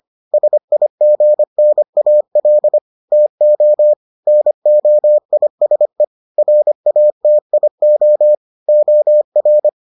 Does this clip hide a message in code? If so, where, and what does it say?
Yes: Morse code from 0.3 to 9.7 s, heard as a keyed tone, about 25 wpm, 600 Hz. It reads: SIGNAL TO NOISE RATIO OR